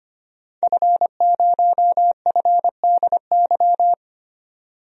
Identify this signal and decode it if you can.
Morse code, heard as a keyed tone, about 25 wpm, 700 Hz. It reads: F0FDY
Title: DXCC-100 Call Signs - 5 Characters - 25wpm